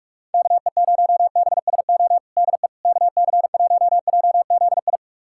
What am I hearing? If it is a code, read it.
Morse code, heard as a keyed tone, about 47 wpm, 700 Hz. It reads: KE0BSO DE KC1JZI
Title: USA Callsign DE Callsign - 6 Characters - 45wpm